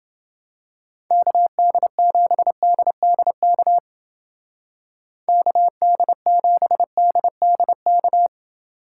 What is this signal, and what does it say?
Morse code, heard as a keyed tone, about 29 wpm, 700 Hz. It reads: KD7DDK KD7DDK